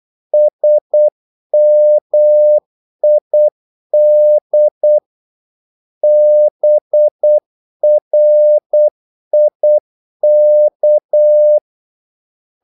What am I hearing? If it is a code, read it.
Morse code, heard as a keyed tone, about 8 wpm, 600 Hz. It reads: SMID BRIK
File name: Korte ord DK 8wpm